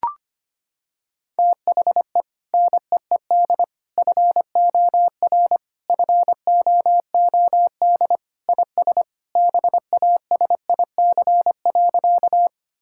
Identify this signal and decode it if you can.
Morse code, heard as a keyed tone, about 25 wpm, 700 Hz. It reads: THE NEED FOR FOOD IS BASIC.
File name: practice / 25wpm / Sentences with 3-Word Phrases - 25wpm